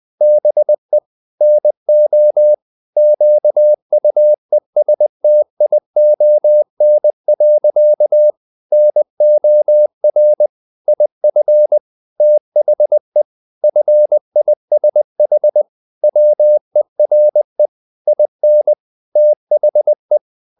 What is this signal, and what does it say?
Morse code, heard as a keyed tone, about 20 wpm, 600 Hz. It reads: BE NO QUESTION. NOR IF THE FISH WERE IN THE
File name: the old man and the sea part 2 20wpm